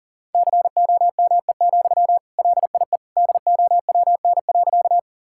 Morse code, heard as a keyed tone, about 40 wpm, 700 Hz. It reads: COME, LIE DOWN.
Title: Sentences from Top 1600 Words 40wpm